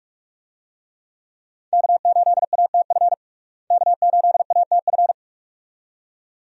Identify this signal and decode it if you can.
Morse code, heard as a keyed tone, about 47 wpm, 700 Hz. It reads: K8ATF K8ATF
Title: USA Call Signs - 5 Characters - Repeated 2x - 45wpm